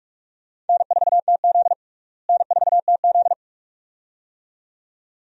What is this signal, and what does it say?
Morse code, heard as a keyed tone, about 47 wpm, 700 Hz. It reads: N4TZ N4TZ
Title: USA Call Signs - 4 Characters - Repeated 2x - 45wpm